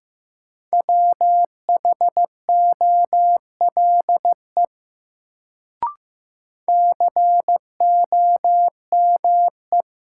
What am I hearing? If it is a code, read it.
Morse code, heard as a keyed tone, about 15 wpm, 700 Hz. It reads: WHOLE COME
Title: Sets of 2 Words - New Word Drill - 15wpm